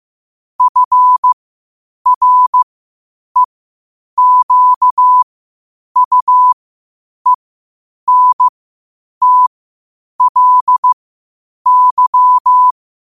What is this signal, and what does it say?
Morse code, heard as a keyed tone, about 15 wpm, spaced as for 8 wpm, 1000 Hz. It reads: FREQUENTLY
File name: qotdfiles / 8wpm